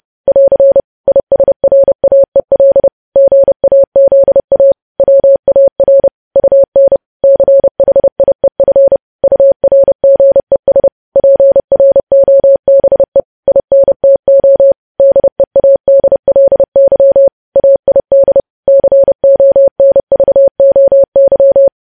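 Morse code, heard as a keyed tone, about 30 wpm, 550 Hz. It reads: <AR> ISRAEL GAZA WAR UN CHIEF URGES PROBE INTO DEADLY AID CONVOY
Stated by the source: Morse news headlines at 30 wpm.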